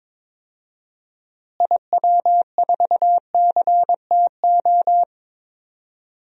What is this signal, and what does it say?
Morse code, heard as a keyed tone, about 22 wpm, 700 Hz. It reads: IW4CTO
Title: DXCC-100 Call Signs - 6 Characters - Repeated 3x - 22wpm